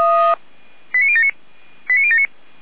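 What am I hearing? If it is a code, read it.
Five-tone selective call (ZVEI), heard as quick runs of five tones, tones of about 70 ms. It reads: 89079 89079
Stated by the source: ZVEI five-tone call received by SDR.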